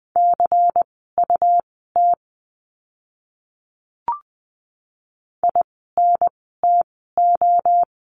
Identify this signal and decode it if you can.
Morse code, heard as a keyed tone, about 20 wpm, spaced as for 14 wpm, 700 Hz. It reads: CUT INTO
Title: 101 Lesson 17f - Letter u - Words 20wpm